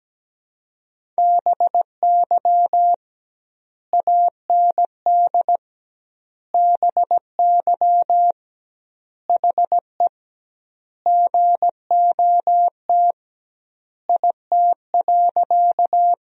Morse code, heard as a keyed tone, about 17 wpm, 700 Hz. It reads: BY AND BY HE GOT IT.
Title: 2x Word-Spacing - Sentences from Top 300 Words - 17wpm